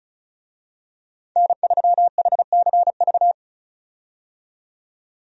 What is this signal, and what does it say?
Morse code, heard as a keyed tone, about 36 wpm, 700 Hz. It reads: N3HCV
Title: USA Call Signs - 5 Characters - 35wpm